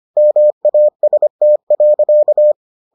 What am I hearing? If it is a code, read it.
Morse code, heard as a keyed tone, about 25 wpm, 600 Hz. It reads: MAST.